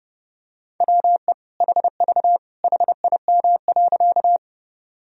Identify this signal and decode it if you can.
Morse code, heard as a keyed tone, about 30 wpm, 700 Hz. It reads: WE HV HIM.
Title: Sentences from Top 100 Words and Abbreviations 30wpm